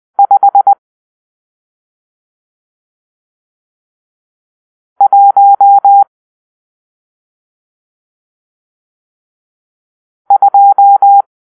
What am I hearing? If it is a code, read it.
Morse code, heard as a keyed tone, about 20 wpm, 800 Hz. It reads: 5 1 2
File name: arrl / Grp-6 Drill-1